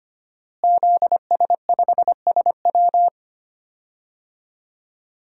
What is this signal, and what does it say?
Morse code, heard as a keyed tone, about 25 wpm, 700 Hz. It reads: ZS5SW
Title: DXCC-100 Call Signs - 5 Characters - Repeated 2x - 25wpm